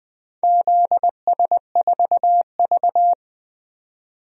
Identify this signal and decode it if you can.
Morse code, heard as a keyed tone, about 20 wpm, 700 Hz. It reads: ZS4V
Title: DXCC-100 Call Signs - 4 Characters - 20wpm